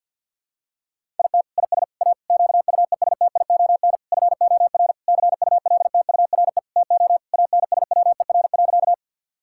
Morse code, heard as a keyed tone, about 48 wpm, 700 Hz. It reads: IT IS A QUESTION FOR CULTURE TO ANSWER.
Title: Sentences from Top 1000 Words - Review Part 6 - 50wpm